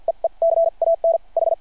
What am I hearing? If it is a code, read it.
Morse code, heard as a keyed tone, about 48 wpm, spaced as for 39 wpm, 660 Hz. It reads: EE CAN H
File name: archived / CRSnet